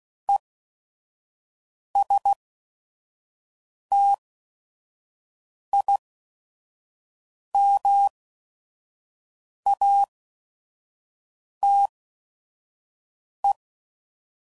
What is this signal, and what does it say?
Morse code, heard as a keyed tone, about 16 wpm, spaced as for 5 wpm, 800 Hz. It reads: ESTIMATE
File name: Prefixes / By First Letter / Es